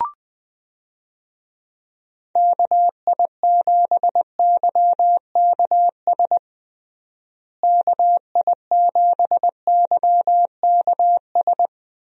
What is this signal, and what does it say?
Morse code, heard as a keyed tone, about 20 wpm, 700 Hz. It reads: KI7YKS KI7YKS
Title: USA Call Signs - 6 Characters - Repeated 3x - 20wpm